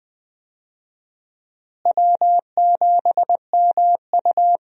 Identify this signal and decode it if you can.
Morse code, heard as a keyed tone, about 20 wpm, 700 Hz. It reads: W7MU
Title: USA Call Signs - 4 Characters - Repeated 2x - 20wpm